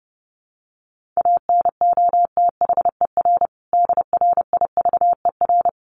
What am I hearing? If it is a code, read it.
Morse code, heard as a keyed tone, about 30 wpm, 700 Hz. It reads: ANOTHER DRIVER
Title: Sentences from Top 1400 Words 30wpm